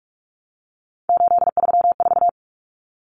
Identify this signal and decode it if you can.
Morse code, heard as a keyed tone, about 44 wpm, 700 Hz. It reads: 834
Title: Common Cut Numbers - 3-Digit 45wpm